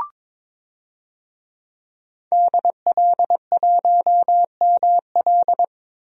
Morse code, heard as a keyed tone, about 22 wpm, 700 Hz. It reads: DL1ML
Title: DXCC-100 Call Signs - 5 Characters - 22wpm